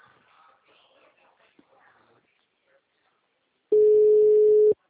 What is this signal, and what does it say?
Ringback tone (Europe, 425 Hz).